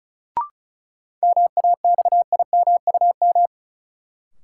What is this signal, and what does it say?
Morse code, heard as a keyed tone, about 34 wpm, 700 Hz. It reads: MAXIMUM
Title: Sentences from Top 1300 Words - New Word Drill Alt - 35wpm